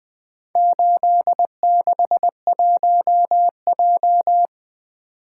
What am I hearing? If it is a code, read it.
Morse code, heard as a keyed tone, about 20 wpm, 700 Hz. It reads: 861J
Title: Four Letter-Number Combinations - 20wpm